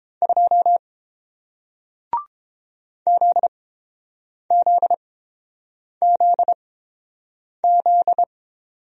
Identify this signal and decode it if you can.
Morse code, heard as a keyed tone, about 30 wpm, spaced as for 7 wpm, 700 Hz. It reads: 2 ZZZZ